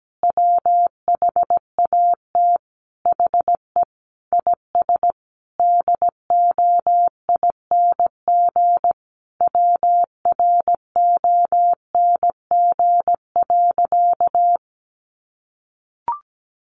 Morse code, heard as a keyed tone, about 17 wpm, 700 Hz. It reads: WHAT HE IS DOING WRONG.
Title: Easy Phrases 17wpm